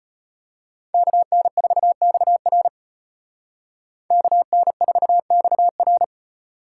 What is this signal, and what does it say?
Morse code, heard as a keyed tone, about 33 wpm, 700 Hz. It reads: KN4XR KN4XR